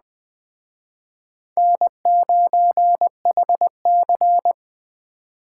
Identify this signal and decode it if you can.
Morse code, heard as a keyed tone, about 24 wpm, 700 Hz. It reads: N9HC
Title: Speed Racing - USA Call Signs 20wpm